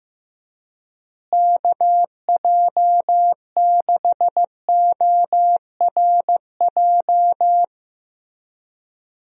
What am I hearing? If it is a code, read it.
Morse code, heard as a keyed tone, about 16 wpm, 700 Hz. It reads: KJ6ORJ
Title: USA Call Signs - 6 Characters - Repeated 2x - 15wpm